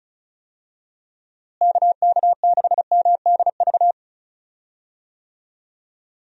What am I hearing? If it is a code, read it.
Morse code, heard as a keyed tone, about 34 wpm, 700 Hz. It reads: KK6MDV
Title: USA Call Signs - 6 Characters - Repeated 3x - 35wpm